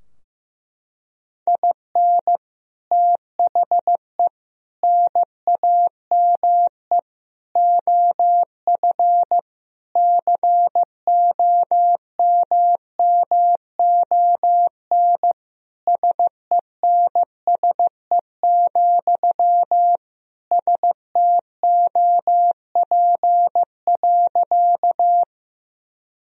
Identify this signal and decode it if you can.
Morse code, heard as a keyed tone, about 15 wpm, 700 Hz. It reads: IN THE NAME OF COMMON SENSE, STOP.
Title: Sentences from Top 1000 Words - Review Part 4 - 15wpm